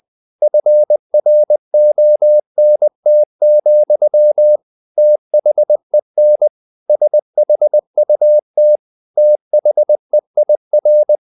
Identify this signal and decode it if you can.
Morse code, heard as a keyed tone, about 20 wpm, 600 Hz. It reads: FRONT, THEN SHUT THEIR
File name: the old man and the sea / the old man and the sea part 1 20wpm